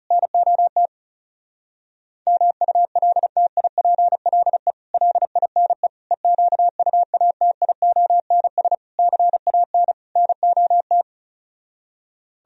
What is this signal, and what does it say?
Morse code, heard as a keyed tone, about 37 wpm, 700 Hz. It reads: NOT MULTIPLE LINE EQUATIONS CAN NOT
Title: Speed-Racing - Sets of 5 Words 30wpm